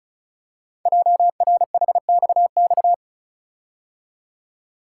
Morse code, heard as a keyed tone, about 36 wpm, 700 Hz. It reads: JRHXX
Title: Code Groups - 5 Letters - 35wpm